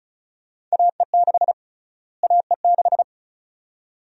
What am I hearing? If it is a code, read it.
Morse code, heard as a keyed tone, about 36 wpm, 700 Hz. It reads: AE6 AE6
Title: USA Call Sign Prefixes - Repeated 2x - 35wpm